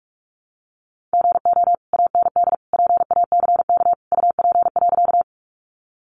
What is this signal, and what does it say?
Morse code, heard as a keyed tone, about 44 wpm, 700 Hz. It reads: GO AND PACK UP.
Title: Sentences from Top 1300 Words 45wpm